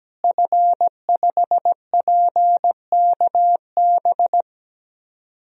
Morse code, heard as a keyed tone, about 17 wpm, 700 Hz. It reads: F5PKB
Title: DXCC-100 Call Signs - 5 Characters - Repeated 3x - 17wpm